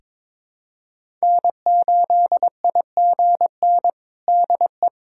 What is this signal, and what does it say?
Morse code, heard as a keyed tone, about 22 wpm, 700 Hz. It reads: N8IGN DE
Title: USA Callsign DE Callsign - 5 Characters - 22wpm